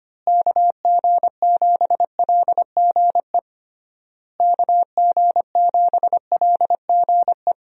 Morse code, heard as a keyed tone, about 25 wpm, 700 Hz. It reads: KG7LGE KG7LGE
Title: USA Call Signs - 6 Characters - Repeated 3x - 25wpm